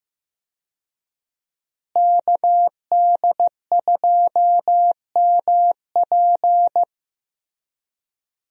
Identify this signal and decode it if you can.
Morse code, heard as a keyed tone, about 15 wpm, 700 Hz. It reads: KD2MP
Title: USA Call Signs - 5 Characters - Repeated 2x - 15wpm